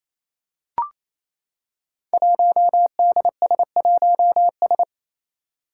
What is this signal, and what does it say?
Morse code, heard as a keyed tone, about 28 wpm, 700 Hz. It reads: 1DS1S
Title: Five Challenging Letter-Number Combinations 28wpm